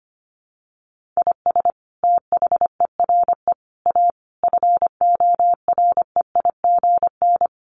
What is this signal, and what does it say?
Morse code, heard as a keyed tone, about 25 wpm, 700 Hz. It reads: IS THERE A FOREIGN
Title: Sentences from Top 1400 Words 25wpm